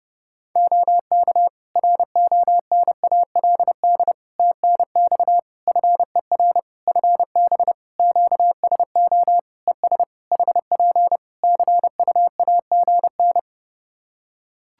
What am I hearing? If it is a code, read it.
Morse code, heard as a keyed tone, about 28 wpm, 700 Hz. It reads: OK RONALD TNX FER FB QSO ES HP CUAGN
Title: Practice QSO - Exchange 4 - Ending - 30wpm